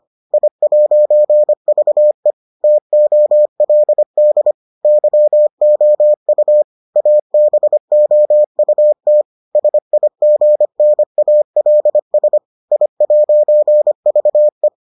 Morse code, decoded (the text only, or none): I'VE TOLD YOU ABOUT SIGNALS I'VE